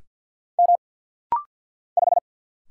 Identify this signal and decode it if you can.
Morse code, heard as a keyed tone, about 48 wpm, 700 Hz. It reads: M 5
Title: Single Letter-Number - Rapid-Fire 50wpm